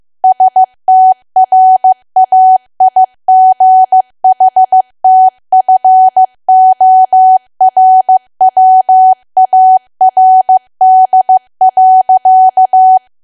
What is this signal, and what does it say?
Morse code, heard as a keyed tone, about 15 wpm, 750 Hz. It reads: STRAIGHTFORWARD.
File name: cwa / W1AW-Level2 / 15WPM